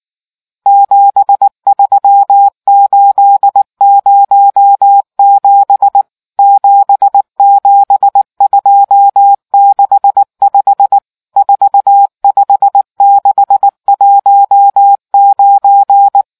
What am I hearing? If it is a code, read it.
Morse code, heard as a keyed tone, about 19 wpm, 800 Hz. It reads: 73807 77265 45619